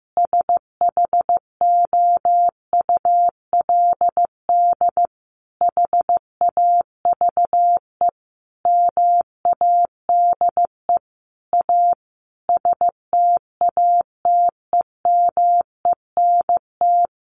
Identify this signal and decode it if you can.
Morse code, heard as a keyed tone, about 15 wpm, 700 Hz. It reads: SHOULD HAVE MADE A STATEMENT
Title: Sentences from Top 1200 Words 15wpm